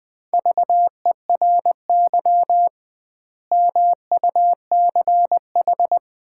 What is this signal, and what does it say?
Morse code, heard as a keyed tone, about 20 wpm, 700 Hz. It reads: VERY MUCH
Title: 2x Word-Spacing - Sets of 5 Words - 20wpm